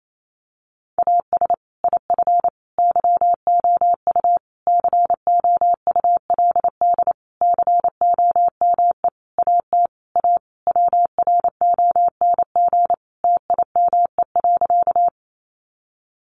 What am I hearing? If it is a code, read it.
Morse code, heard as a keyed tone, about 28 wpm, 700 Hz. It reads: AS IF YOU COULD COME AT A WRONG TIME.
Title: Sentences from Top 900 Words 28wpm